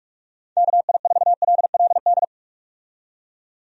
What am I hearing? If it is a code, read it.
Morse code, heard as a keyed tone, about 43 wpm, 700 Hz. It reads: KI4LLD
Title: USA Call Signs - 6 Characters - 45wpm